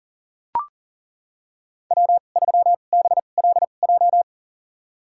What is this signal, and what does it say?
Morse code, heard as a keyed tone, about 39 wpm, spaced as for 29 wpm, 700 Hz. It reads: W3BLJ
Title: USA Call Signs - 5 Characters - 2x Character Spacing - 40wpm